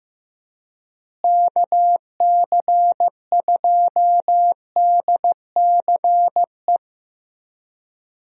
Morse code, heard as a keyed tone, about 15 wpm, 700 Hz. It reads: KC2DCE
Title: USA Call Signs - 6 Characters - Repeated 3x - 15wpm